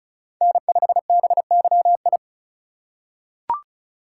Morse code, heard as a keyed tone, about 36 wpm, 700 Hz. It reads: N5BYI
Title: USA Call Signs - 3x Character Spacing - 35wpm